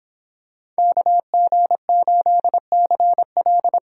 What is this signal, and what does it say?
Morse code, heard as a keyed tone, about 26 wpm, 700 Hz. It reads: KG8CL